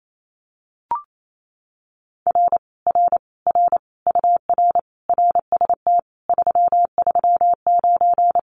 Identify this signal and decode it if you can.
Morse code, heard as a keyed tone, about 28 wpm, 700 Hz. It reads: R R R UR RST 339
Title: Common QSO Phrases Encore 28wpm